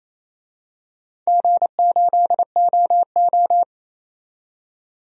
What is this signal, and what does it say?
Morse code, heard as a keyed tone, about 28 wpm, 700 Hz. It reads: G8OO
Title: DXCC-100 Call Signs - 4 Characters - 28wpm